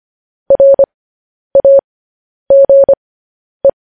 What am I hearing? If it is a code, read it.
Morse code, heard as a keyed tone, about 25 wpm, spaced as for 10 wpm, 550 Hz. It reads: RAGE
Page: news-headlines-in-morse-code-at-10-wpm